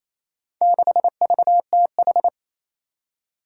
Morse code, heard as a keyed tone, about 28 wpm, 700 Hz. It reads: 6VTH
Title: Four Challenging Letter-Number Combinations 28wpm